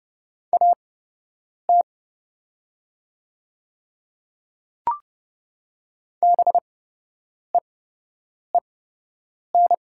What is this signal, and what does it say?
Morse code, heard as a keyed tone, about 30 wpm, spaced as for 8 wpm, 700 Hz. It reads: AT BEEN
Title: 8x Character Spacing - Top 100 Words - 30wpm